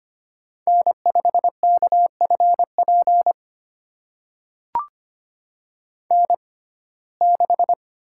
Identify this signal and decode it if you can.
Morse code, heard as a keyed tone, about 25 wpm, spaced as for 18 wpm, 700 Hz. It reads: N5KFP N6
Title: USA Call Signs - 6x Character Spacing - 25wpm